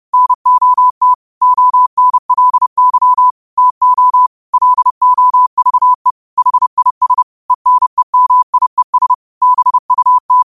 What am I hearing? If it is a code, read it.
Morse code, heard as a keyed tone, about 30 wpm, 1000 Hz. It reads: NOT ONLY TO LOVE HIS ENEMIES BUT
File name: qotdfiles / 30wpm